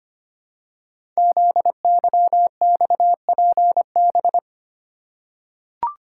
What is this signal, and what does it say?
Morse code, heard as a keyed tone, about 25 wpm, 700 Hz. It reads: ZYXPB
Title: Code Groups - 5 Letter:Numbers - 25wpm